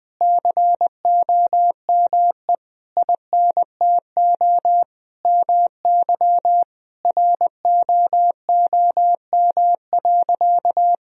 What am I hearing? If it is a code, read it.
Morse code, heard as a keyed tone, about 21 wpm, 700 Hz. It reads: COME INTO MY ROOM.
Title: Sentences from Top 1000 Words - Review Part 1 - 20wpm